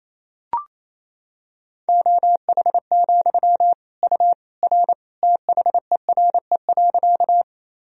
Morse code, heard as a keyed tone, about 28 wpm, 700 Hz. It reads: OH, U R THERE.
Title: Sentences from Top 500 Words and Abbreviations 28wpm